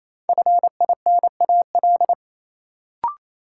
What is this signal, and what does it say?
Morse code, heard as a keyed tone, about 28 wpm, 700 Hz. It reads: FINAL
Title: Sentences from Top 800 Words - New Word Drill - 28wpm